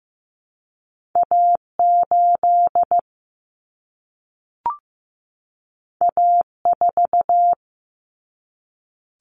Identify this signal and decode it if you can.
Morse code, heard as a keyed tone, about 15 wpm, 700 Hz. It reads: A8 A4
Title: Common Cut Numbers - 2-Digit 15wpm